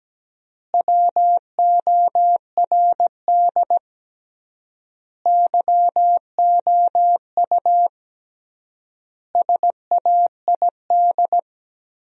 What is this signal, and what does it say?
Morse code, heard as a keyed tone, about 17 wpm, 700 Hz. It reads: WORD YOU SAID
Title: 3x Word-Spacing - Sentences from Top 400 Words - 17wpm